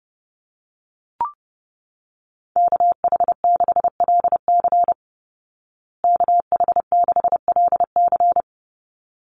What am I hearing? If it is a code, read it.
Morse code, heard as a keyed tone, about 30 wpm, 700 Hz. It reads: KH6LC KH6LC
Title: Call Signs Repeated 3x - 30wpm